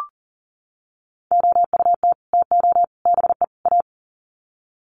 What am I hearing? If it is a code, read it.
Morse code, heard as a keyed tone, about 38 wpm, 700 Hz. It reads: OUT TO BE A
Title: Sets of 4 Words 40wpm